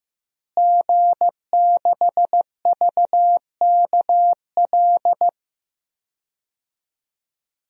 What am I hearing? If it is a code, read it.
Morse code, heard as a keyed tone, about 15 wpm, 700 Hz. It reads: G6VKL
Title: DXCC-100 Call Signs - 5 Characters - Repeated 3x - 15wpm